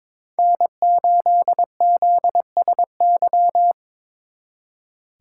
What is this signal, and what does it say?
Morse code, heard as a keyed tone, about 26 wpm, 700 Hz. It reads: N8ZSY
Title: Speed Racing - USA 1x3 Call Signs 22wpm